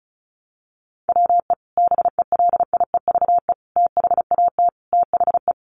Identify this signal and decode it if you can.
Morse code, heard as a keyed tone, about 34 wpm, 700 Hz. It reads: WE BELIEVE THAT THE
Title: Sentences from Top 1300 Words 35wpm